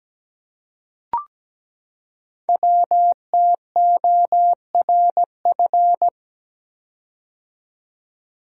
Morse code, heard as a keyed tone, about 17 wpm, 700 Hz. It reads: WTORF
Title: Code Groups - 5 Letter:Numbers - 17wpm